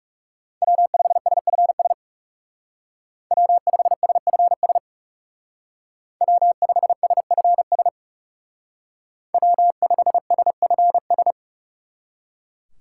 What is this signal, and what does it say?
Morse code, heard as a keyed tone, about 37 wpm, 700 Hz. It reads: W5SFS W5SFS W5SFS W5SFS
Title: Speed Racing - USA 1x3 Call Signs 30wpm